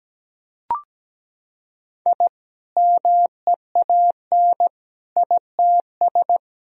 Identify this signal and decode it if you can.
Morse code, heard as a keyed tone, about 17 wpm, 700 Hz. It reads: I MEAN ITS
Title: Sentences from Top 1000 Words - Review Part 4 - 17wpm